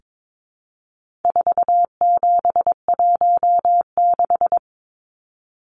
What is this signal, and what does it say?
Morse code, heard as a keyed tone, about 22 wpm, 700 Hz. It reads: 4716
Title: Numbers - 4-Digit 22wpm